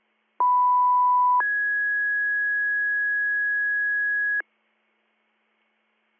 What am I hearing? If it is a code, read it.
Two-tone page, heard as a one-second tone, then a three-second tone, about 990 then 1650 Hz. It is heard from 0.4 to 4.4 s.